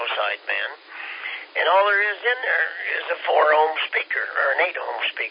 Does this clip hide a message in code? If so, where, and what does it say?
Yes: Emergency Alert System (SAME) tones from 1.1 to 4.3 s, heard as harsh buzzing bursts, with speech.